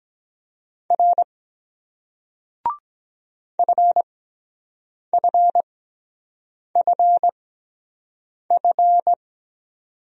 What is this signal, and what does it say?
Morse code, heard as a keyed tone, about 23 wpm, spaced as for 7 wpm, 700 Hz. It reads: R FFFF